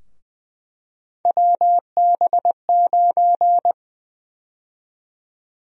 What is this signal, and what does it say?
Morse code, heard as a keyed tone, about 21 wpm, 700 Hz. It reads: WB9
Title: USA Call Sign Prefixes - Repeated 2x - 20wpm